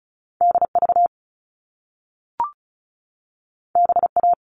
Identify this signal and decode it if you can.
Morse code, heard as a keyed tone, about 36 wpm, 690 Hz. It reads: DV BA
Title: All Cut Numbers - 2-Digit 35wpm